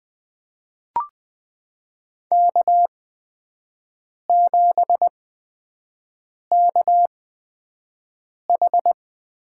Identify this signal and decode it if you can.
Morse code, heard as a keyed tone, about 21 wpm, spaced as for 5 wpm, 700 Hz. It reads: K7KH
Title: USA Call Signs - 8x Character Spacing - 20wpm